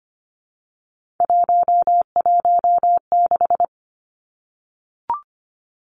Morse code, heard as a keyed tone, about 25 wpm, 700 Hz. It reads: 116
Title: Numbers - 3-Digit 25wpm